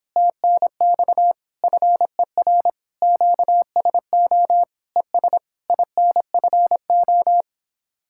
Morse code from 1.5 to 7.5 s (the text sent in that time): FER QSO ES INFO